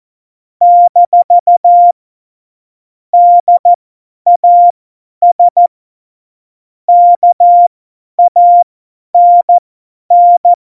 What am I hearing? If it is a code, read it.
Morse code, heard as a keyed tone, about 14 wpm, spaced as for 10 wpm, 700 Hz. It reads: - DAS KANN
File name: Morsekurs / Witze / Witz 10. Oktober 17, Tempo 50